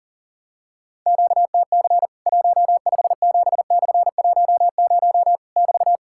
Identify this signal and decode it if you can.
Morse code, heard as a keyed tone, about 40 wpm, 700 Hz. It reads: QTC 157/10 <BT>
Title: Worked All Europe DX Contest - QTC Exchange - 40wpm